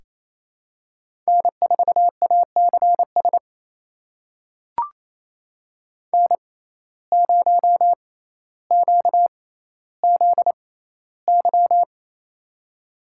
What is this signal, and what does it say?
Morse code, heard as a keyed tone, about 28 wpm, spaced as for 14 wpm, 700 Hz. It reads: N4ACS N0QZY
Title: USA Call Signs - 5 Characters - 6x Character Spacing - 28wpm